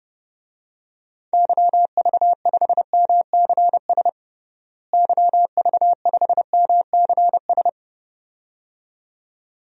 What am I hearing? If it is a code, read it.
Morse code, heard as a keyed tone, about 30 wpm, 700 Hz. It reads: YV5MCS YV5MCS
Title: DXCC-100 Call Signs - 6 Characters - Repeated 2x - 30wpm